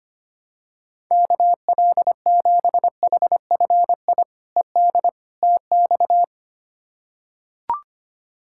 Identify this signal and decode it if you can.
Morse code, heard as a keyed tone, about 25 wpm, 700 Hz. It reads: KL7HFI ED TX
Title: K1USN SST QSO Practice - 25wpm